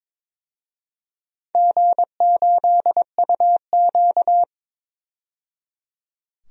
Morse code, heard as a keyed tone, about 22 wpm, 700 Hz. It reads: G8UQ
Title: DXCC-100 Call Signs - 4 Characters - 22wpm